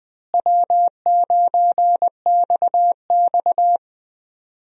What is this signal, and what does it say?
Morse code, heard as a keyed tone, about 20 wpm, 700 Hz. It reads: W9XX
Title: USA Call Signs - 4 Characters - Repeated 3x - 20wpm